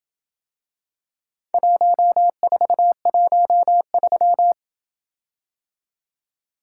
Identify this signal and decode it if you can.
Morse code, heard as a keyed tone, about 27 wpm, 700 Hz. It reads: 1413